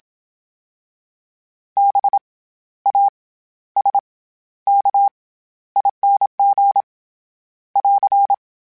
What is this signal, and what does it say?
Morse code, heard as a keyed tone, about 27 wpm, 800 Hz. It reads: B A S K ING <AR>
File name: Special INGs-mp3s / Special INGs-mp3s / SlowToFaster ing @25